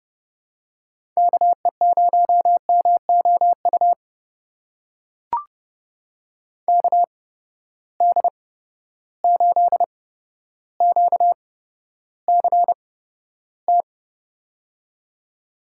Morse code, heard as a keyed tone, about 29 wpm, spaced as for 13 wpm, 700 Hz. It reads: KE0MOU KD8QCT
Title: USA Call Signs - 6 Characters - 8x Character Spacing - 30wpm